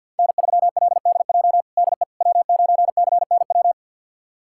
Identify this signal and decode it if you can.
Morse code, heard as a keyed tone, about 46 wpm, 700 Hz. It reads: N3LDJ DE W9CNW
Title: USA Callsign DE Callsign - 5 Characters - 50wpm